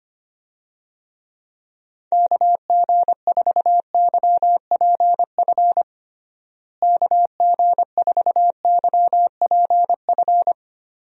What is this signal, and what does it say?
Morse code, heard as a keyed tone, about 25 wpm, 700 Hz. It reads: KG4YPF KG4YPF
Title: USA Call Signs - 6 Characters - Repeated 3x - 25wpm